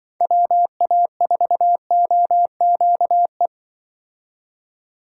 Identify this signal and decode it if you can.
Morse code, heard as a keyed tone, about 24 wpm, 700 Hz. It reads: WA4OQE
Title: Speed Racing - USA Call Signs 20wpm